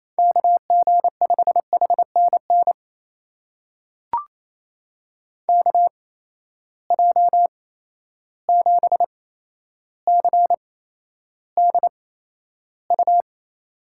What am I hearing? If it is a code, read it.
Morse code, heard as a keyed tone, about 28 wpm, spaced as for 12 wpm, 700 Hz. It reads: KG5HNN KJ7CDU
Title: USA Call Signs - 6 Characters - 8x Character Spacing - 28wpm